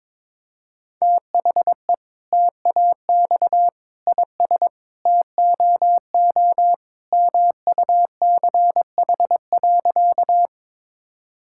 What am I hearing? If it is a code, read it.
Morse code, heard as a keyed tone, about 21 wpm, 700 Hz. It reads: THE TAX IS TOO MUCH.
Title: Sentences from Top 1000 Words - Review Part 5 - 22wpm